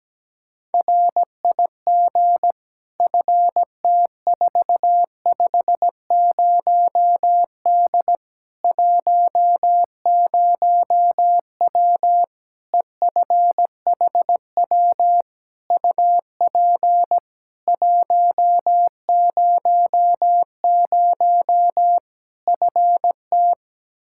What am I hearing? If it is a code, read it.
Morse code, heard as a keyed tone, about 17 wpm, 700 Hz. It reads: RIG FT450D 10W EFHW UP 100 FT
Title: K1USN SST QSO Practice 17wpm